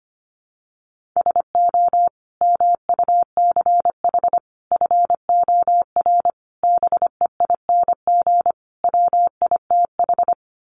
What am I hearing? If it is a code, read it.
Morse code, heard as a keyed tone, about 25 wpm, 700 Hz. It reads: SO MUCH FOR BEING WITH
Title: Sets of 5 Words Encore 25wpm